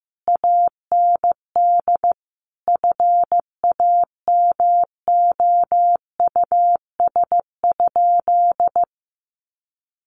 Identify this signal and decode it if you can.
Morse code, heard as a keyed tone, about 15 wpm, 700 Hz. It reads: AND FAMOUS?
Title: Sentences from Top 1500 Words 15wpm